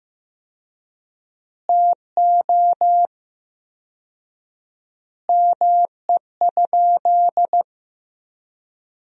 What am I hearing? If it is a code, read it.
Morse code, heard as a keyed tone, about 15 wpm, 700 Hz. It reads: TO ME?
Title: 4x Word-Spacing - Sentences from Top 500 Words - 15wpm